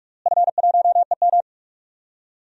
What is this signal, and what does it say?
Morse code, heard as a keyed tone, about 43 wpm, 700 Hz. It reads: F1EM
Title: DXCC-100 Call Signs - 4 Characters - Repeated 3x - 45wpm